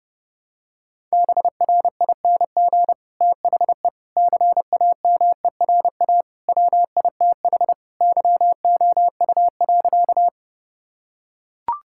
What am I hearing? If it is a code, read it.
Morse code, heard as a keyed tone, about 30 wpm, 700 Hz. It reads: BRING THE CAMERA WITH YOU.
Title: Sentences from Top 1000 Words - Review Part 5 - 30wpm